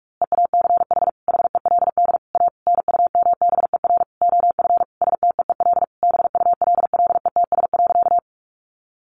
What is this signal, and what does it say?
Morse code, heard as a keyed tone, about 44 wpm, 700 Hz. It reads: EACH HELD A NUMBER OF STEEL BULLETS.
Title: A Sound Of Thunder 45wpm